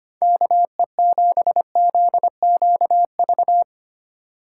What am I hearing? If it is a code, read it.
Morse code, heard as a keyed tone, about 25 wpm, 690 Hz. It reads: KE7ZQV
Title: USA Call Signs - 6 Characters - 2x Character Spacing - 25wpm